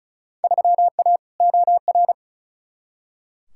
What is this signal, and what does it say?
Morse code, heard as a keyed tone, about 34 wpm, 700 Hz. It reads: 3A OR
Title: Field Day Exchanges Encore - 35wpm